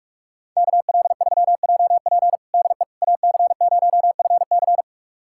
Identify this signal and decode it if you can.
Morse code, heard as a keyed tone, about 46 wpm, 700 Hz. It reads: KL3JP DE AC0FC
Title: USA Callsign DE Callsign - 5 Characters - 45wpm